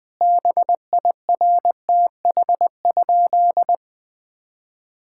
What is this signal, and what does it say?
Morse code, heard as a keyed tone, about 20 wpm, 700 Hz. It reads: BIRTH?
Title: Sentences from Top 1000 Words - Review Part 5 - 20wpm